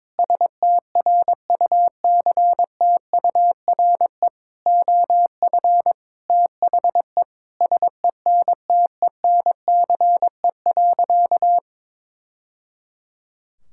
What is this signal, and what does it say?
Morse code, heard as a keyed tone, about 22 wpm, 700 Hz. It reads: STRUCTURE OF THE SENTENCE.
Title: Sentences from Top 1000 Words - Review Part 6 - 22wpm